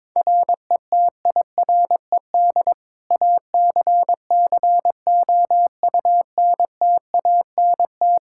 Morse code, heard as a keyed tone, about 22 wpm, 700 Hz. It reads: RETIRED ACCOUNTANT
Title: Practice QSO - Exchange 3 - More Info - 22wpm